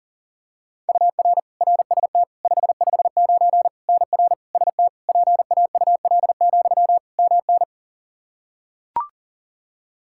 Morse code, heard as a keyed tone, about 37 wpm, 700 Hz. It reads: UR RST 559 NR ST PAUL, MN
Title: Practice QSO - Exchange 1 - Beginning - 40wpm